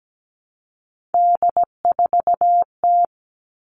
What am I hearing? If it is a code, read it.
Morse code, heard as a keyed tone, about 17 wpm, 700 Hz. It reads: D4T
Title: All Cut Numbers - 3-Digit 17wpm